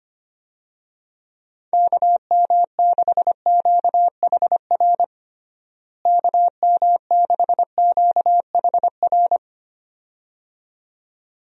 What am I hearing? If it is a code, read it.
Morse code, heard as a keyed tone, about 25 wpm, 700 Hz. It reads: KM6QHR KM6QHR
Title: USA Call Signs - 6 Characters - Repeated 2x - 25wpm